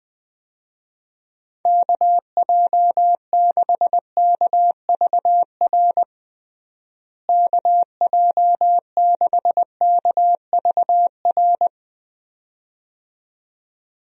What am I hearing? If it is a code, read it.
Morse code, heard as a keyed tone, about 20 wpm, 700 Hz. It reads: KJ6KVR KJ6KVR
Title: USA Call Signs - 6 Characters - Repeated 2x - 20wpm